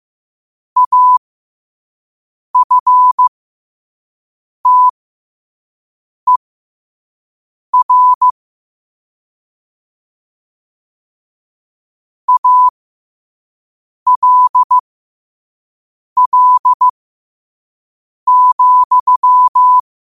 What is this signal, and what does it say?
Morse code, heard as a keyed tone, about 15 wpm, spaced as for 5 wpm, 1000 Hz. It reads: AFTER ALL,
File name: qotdfiles / 5wpm